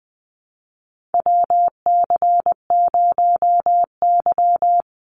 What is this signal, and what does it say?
Morse code, heard as a keyed tone, about 20 wpm, 700 Hz. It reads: WC0Y
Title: NA SOTA Chaser Callsigns 20wpm